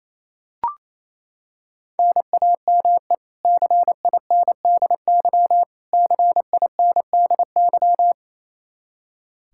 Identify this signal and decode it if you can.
Morse code, heard as a keyed tone, about 28 wpm, 700 Hz. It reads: NAME CINDY CINDY
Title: K1USN SST QSO Practice 28wpm